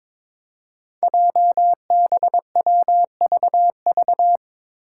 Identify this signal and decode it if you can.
Morse code, heard as a keyed tone, about 22 wpm, 700 Hz. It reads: JBWVV
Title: Five Challenging Letter-Number Combinations 22wpm